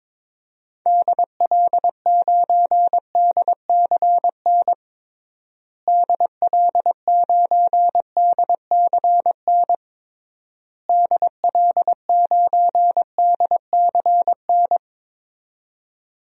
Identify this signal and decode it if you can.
Morse code, heard as a keyed tone, about 21 wpm, 700 Hz. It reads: DL9DCN DL9DCN DL9DCN
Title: DXCC-100 Call Signs - 6 Characters - Repeated 3x - 22wpm